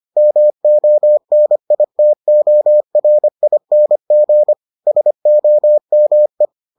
Morse code, heard as a keyed tone, about 25 wpm, 600 Hz. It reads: MONITORING SOME